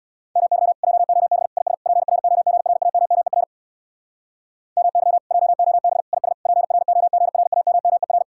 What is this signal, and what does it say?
Morse code, heard as a keyed tone, about 65 wpm, 700 Hz. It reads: MY JOB IS PROGRAMMER MY JOB IS PROGRAMMER